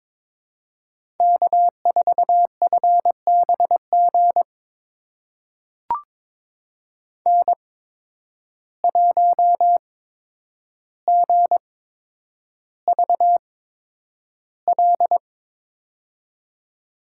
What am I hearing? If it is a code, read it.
Morse code, heard as a keyed tone, about 22 wpm, spaced as for 9 wpm, 700 Hz. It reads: K4FBG N1GVL
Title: USA Call Signs - 5 Characters - 8x Character Spacing - 22wpm